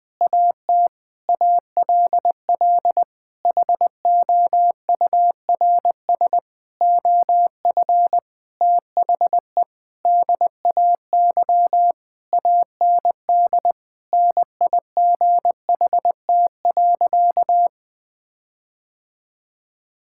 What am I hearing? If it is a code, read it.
Morse code, heard as a keyed tone, about 20 wpm, 700 Hz. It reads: AT ALL HOURS OF THE DAY AND NIGHT.
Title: Sentences with 8-Word Phrases - Spoken Beginning - 20wpm